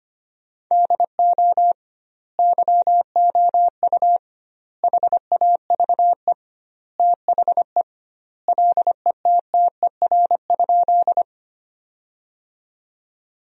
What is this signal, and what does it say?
Morse code, heard as a keyed tone, about 25 wpm, 700 Hz. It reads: DO YOU HAVE THE LETTER?
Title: 2x Word-Spacing - Sentences from Top 300 Words - 25wpm